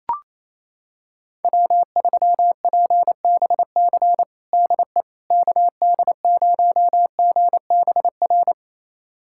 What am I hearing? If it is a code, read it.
Morse code, heard as a keyed tone, about 28 wpm, 700 Hz. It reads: W3PBC DE KD0GBR
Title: USA Callsign DE Callsign 28wpm